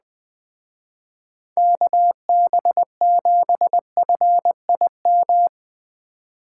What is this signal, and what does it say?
Morse code, heard as a keyed tone, about 20 wpm, 700 Hz. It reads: KB7FIM
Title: Speed Racing - USA 2x3 Call Signs 17wpm